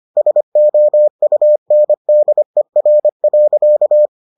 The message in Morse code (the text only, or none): SOUNDER.